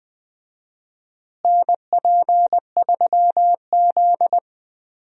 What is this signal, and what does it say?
Morse code, heard as a keyed tone, about 20 wpm, 700 Hz. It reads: NP3Z
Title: USA Call Signs - 4 Characters - Repeated 3x - 20wpm